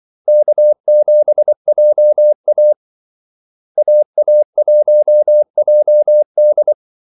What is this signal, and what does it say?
Morse code, heard as a keyed tone, about 24 wpm, 600 Hz. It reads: K7JA AA1JD
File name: Callsigns-24wpm